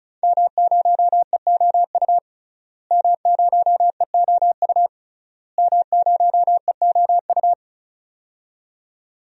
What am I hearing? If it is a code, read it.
Morse code, heard as a keyed tone, about 34 wpm, 700 Hz. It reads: M0EOU M0EOU M0EOU
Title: DXCC-100 Call Signs - 5 Characters - Repeated 3x - 35wpm